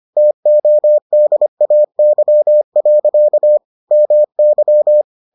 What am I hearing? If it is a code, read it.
Morse code, heard as a keyed tone, about 25 wpm, 600 Hz. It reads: TODAY. MY